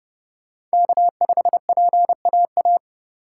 Morse code, heard as a keyed tone, about 32 wpm, 700 Hz. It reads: K5PAA